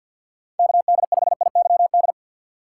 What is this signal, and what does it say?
Morse code, heard as a keyed tone, about 52 wpm, 700 Hz. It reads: KD5IYD
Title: USA Call Signs - 6 Characters - 3x Character Spacing - 50wpm